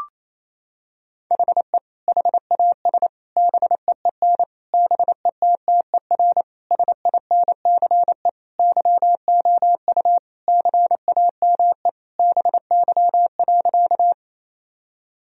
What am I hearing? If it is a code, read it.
Morse code, heard as a keyed tone, about 28 wpm, 700 Hz. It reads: HE HAS BEEN BETTER SINCE YOU CAME BY.
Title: Sentences from Top 1000 Words - Review Part 1 - 28wpm